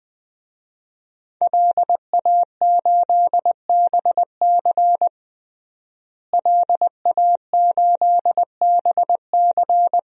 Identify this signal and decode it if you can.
Morse code, heard as a keyed tone, about 20 wpm, 700 Hz. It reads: LA8BC LA8BC
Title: DXCC-100 Call Signs - 5 Characters - Repeated 2x - 20wpm